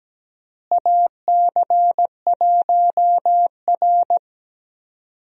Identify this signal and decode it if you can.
Morse code, heard as a keyed tone, about 17 wpm, 700 Hz. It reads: AC1R
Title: USA Call Signs - 4 Characters - Repeated 3x - 17wpm